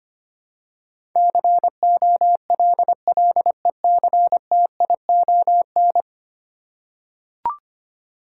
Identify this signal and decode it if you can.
Morse code, heard as a keyed tone, about 25 wpm, 700 Hz. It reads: COLLECTION
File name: 10 Letter Words - First 3 Letters Given - 25wpm